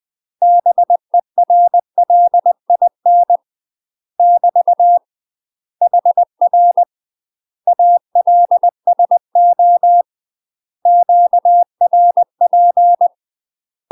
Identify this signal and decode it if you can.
Morse code, heard as a keyed tone, about 20 wpm, 700 Hz. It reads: BERLIN <BT> HR ALSO QRP
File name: UK MASTER QSOs TRAINING / 20WPM / UK MASTER QSOs 0063